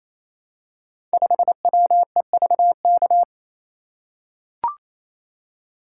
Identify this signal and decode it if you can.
Morse code, heard as a keyed tone, about 27 wpm, 700 Hz. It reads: 5WEVK
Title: Code Groups - 5 Letter:Numbers - 28wpm